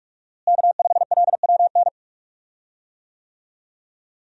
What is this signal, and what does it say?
Morse code, heard as a keyed tone, about 46 wpm, 700 Hz. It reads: K5LWN
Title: USA Call Signs - 5 Characters - Repeated 3x - 45wpm